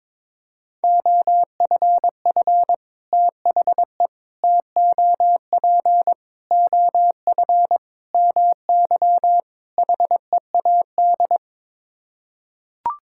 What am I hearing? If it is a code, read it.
Morse code, heard as a keyed tone, about 22 wpm, 700 Hz. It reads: OFF THE TOP OF MY HEAD
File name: practice / 22wpm / Sets of 6 Words - 22wpm